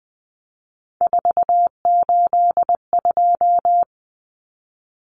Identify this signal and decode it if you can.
Morse code, heard as a keyed tone, about 20 wpm, 700 Hz. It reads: 482
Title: Common Cut Numbers - 3-Digit 20wpm